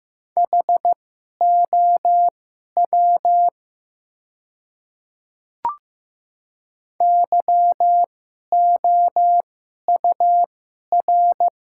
Morse code, heard as a keyed tone, about 15 wpm, spaced as for 11 wpm, 700 Hz. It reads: HOW YOUR
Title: 2x Character Spacing - Top 200 Words - 15wpm